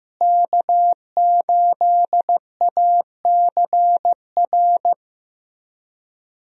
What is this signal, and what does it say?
Morse code, heard as a keyed tone, about 15 wpm, 700 Hz. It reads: K8ACR
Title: USA Call Signs - 5 Characters - Repeated 2x - 15wpm